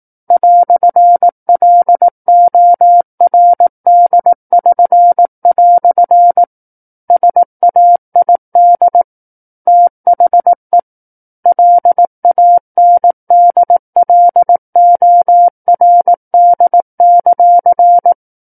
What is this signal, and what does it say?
Morse code, heard as a keyed tone, about 18 wpm, 700 Hz. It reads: "LORD<SN>" SAID THE LANDLORD;
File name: 18 WPM